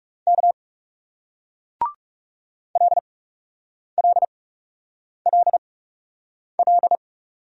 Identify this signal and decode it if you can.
Morse code, heard as a keyed tone, about 40 wpm, spaced as for 8 wpm, 700 Hz. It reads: K LLLL